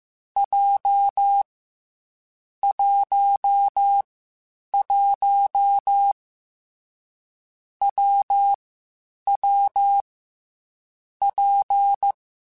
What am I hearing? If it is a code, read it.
Morse code, heard as a keyed tone, about 15 wpm, 800 Hz. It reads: J 1 1 W W P
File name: Difficult Combos / Difficult Combos-mp3s / WPJ WPJ1 @13